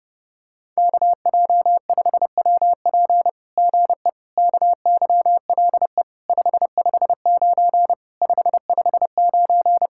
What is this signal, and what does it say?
Morse code, heard as a keyed tone, about 30 wpm, 700 Hz. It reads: KJ5WP GE KYLE 559 559
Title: SOTA QSO Practice 30wpm